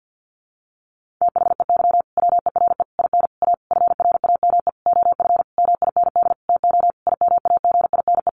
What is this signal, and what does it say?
Morse code, heard as a keyed tone, about 48 wpm, 700 Hz. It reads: THEY WERE IN A FRAME OF MIND TO IMAGINE
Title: Sentences from Top 1400 Words 50wpm